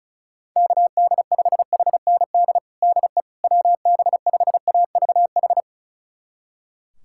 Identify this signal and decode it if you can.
Morse code, heard as a keyed tone, about 36 wpm, 700 Hz. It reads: KD5HND DE WB5AVH
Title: USA Callsign DE Callsign - 6 Characters - 35wpm